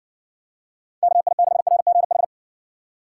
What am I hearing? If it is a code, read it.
Morse code, heard as a keyed tone, about 60 wpm, 700 Hz. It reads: KI6RGH